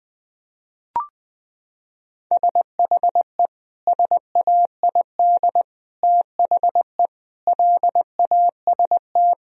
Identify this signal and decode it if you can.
Morse code, heard as a keyed tone, about 20 wpm, 700 Hz. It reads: SHE SAID THE LAST
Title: Sentences from Top 1000 Words - Review Part 1 - 20wpm